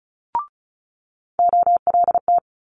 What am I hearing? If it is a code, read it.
Morse code, heard as a keyed tone, about 36 wpm, 700 Hz. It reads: OLT
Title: Multi-letter Phonemes Quick Fire 35wpm